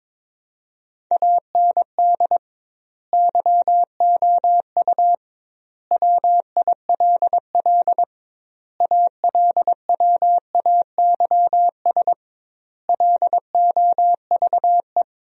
Morse code, heard as a keyed tone, about 22 wpm, 700 Hz. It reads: AND YOU WILL ALWAYS LOVE
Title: 2x Word-Spacing - Sentences from Top 400 Words - 22wpm